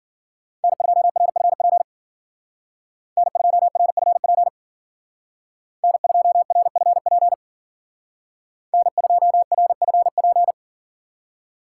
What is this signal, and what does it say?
Morse code, heard as a keyed tone, about 52 wpm, 700 Hz. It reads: N2RFP N2RFP N2RFP N2RFP